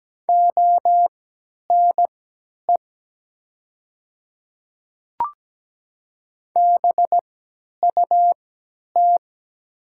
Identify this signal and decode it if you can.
Morse code, heard as a keyed tone, about 17 wpm, spaced as for 10 wpm, 700 Hz. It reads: ONE BUT
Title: 3x Character Spacing - Top 100 Words - 17wpm